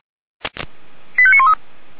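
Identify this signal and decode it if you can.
Five-tone selective call (ZVEI), heard as quick runs of five tones, tones of about 70 ms. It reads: 85913
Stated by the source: Five-tone ZVEI selective call received by SDR, from the audio server of the Feuerwehr Orlamünde fire brigade.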